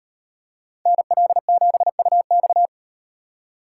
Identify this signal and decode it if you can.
Morse code, heard as a keyed tone, about 36 wpm, 700 Hz. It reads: NL7UX